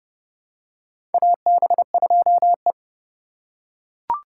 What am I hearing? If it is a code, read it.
Morse code, heard as a keyed tone, about 30 wpm, 700 Hz. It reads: AB2E
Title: USA Call Signs - 4 Characters - 4x Character Spacing - 30wpm